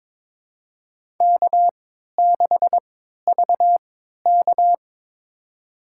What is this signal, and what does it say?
Morse code, heard as a keyed tone, about 22 wpm, spaced as for 13 wpm, 700 Hz. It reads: K6VK
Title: USA Call Signs - 3x Character Spacing - 22wpm